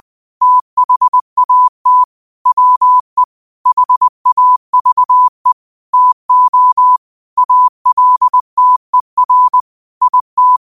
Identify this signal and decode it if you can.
Morse code, heard as a keyed tone, about 20 wpm, 1000 Hz. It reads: THAT WE HAVE TO ALTER IT
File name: qotdfiles / 20wpm